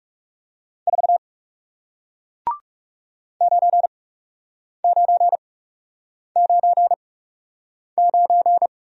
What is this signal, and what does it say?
Morse code, heard as a keyed tone, about 38 wpm, spaced as for 8 wpm, 700 Hz. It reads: 4 9999